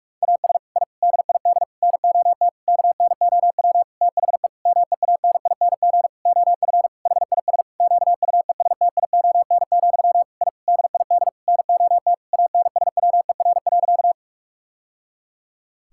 Morse code, heard as a keyed tone, about 41 wpm, 700 Hz. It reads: AS I DID NOT KNOW THE MEANING OF HIS QUESTION, I DID NOT ANSWER.
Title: Sentences from Top 1000 Words - Review Part 6 - 45wpm